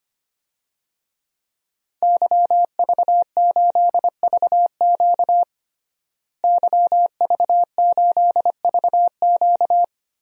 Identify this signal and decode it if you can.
Morse code, heard as a keyed tone, about 25 wpm, 700 Hz. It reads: YV8VQ YV8VQ
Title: DXCC-100 Call Signs - 5 Characters - Repeated 2x - 25wpm